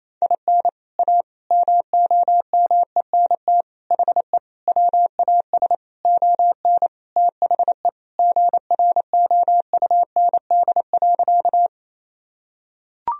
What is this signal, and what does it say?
Morse code, heard as a keyed tone, about 27 wpm, 700 Hz. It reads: IN A MOMENT HE WAS ON THE GROUND.
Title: Sentences from Top 1000 Words - Review Part 2 - 28wpm